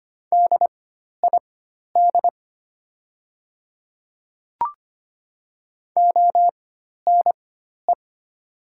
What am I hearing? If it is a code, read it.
Morse code, heard as a keyed tone, about 25 wpm, spaced as for 12 wpm, 690 Hz. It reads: DID ONE
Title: 4x Character Spacing - Top 100 Words - 25wpm